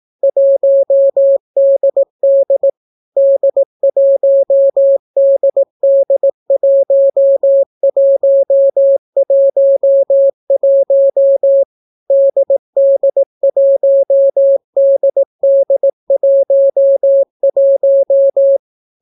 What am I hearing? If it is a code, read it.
Morse code, heard as a keyed tone, about 18 wpm, 550 Hz. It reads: 1DD D1DD1111 DD1DD11